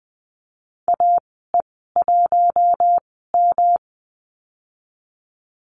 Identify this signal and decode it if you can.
Morse code, heard as a keyed tone, about 20 wpm, spaced as for 15 wpm, 700 Hz. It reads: AE1M
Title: 128 Lesson 21g - Number 6 - Call Signs 20wpm